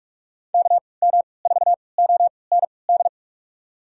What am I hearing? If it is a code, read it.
Morse code, heard as a keyed tone, about 45 wpm, spaced as for 25 wpm, 700 Hz. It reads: KM4OND
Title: USA Call Signs - 6 Characters - 3x Character Spacing - 45wpm